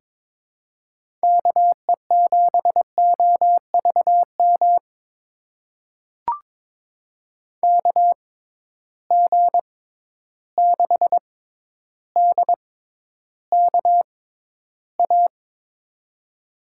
Morse code, heard as a keyed tone, about 22 wpm, spaced as for 11 wpm, 700 Hz. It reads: KE7OVM KG6DKA
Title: USA Call Signs - 6 Characters - 6x Character Spacing - 22wpm